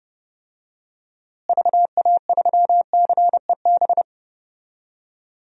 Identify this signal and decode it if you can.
Morse code, heard as a keyed tone, about 32 wpm, 700 Hz. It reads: VA3CEB
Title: DXCC-100 Call Signs - 6 Characters - Repeated 2x - 30wpm